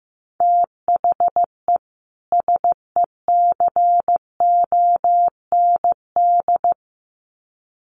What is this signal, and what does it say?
Morse code, heard as a keyed tone, about 15 wpm, 700 Hz. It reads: THE SECOND
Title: Sets of 2 Words 15wpm